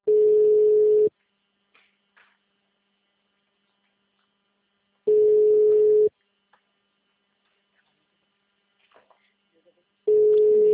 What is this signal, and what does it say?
Ringback tone (Europe, 425 Hz), with speech.